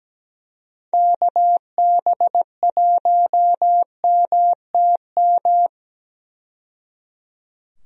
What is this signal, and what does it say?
Morse code, heard as a keyed tone, about 17 wpm, 700 Hz. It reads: KB1MTM
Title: USA Call Signs - 6 Characters - 17wpm